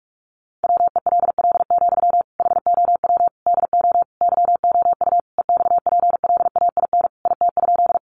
Morse code, heard as a keyed tone, about 44 wpm, 700 Hz. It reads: WELL, HOW DO YOU EXPLAIN IT?
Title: Sentences from Top 1100 Words 45wpm